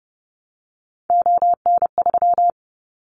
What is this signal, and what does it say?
Morse code, heard as a keyed tone, about 30 wpm, 700 Hz. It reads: ON3
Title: Callsign Prefixes for Top 25 DXCC - Repeated 3x - 30wpm